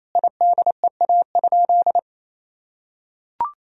Morse code, heard as a keyed tone, about 28 wpm, 700 Hz. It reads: IDEA?
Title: Sentences from Top 1000 Words - Review Part 4 - 28wpm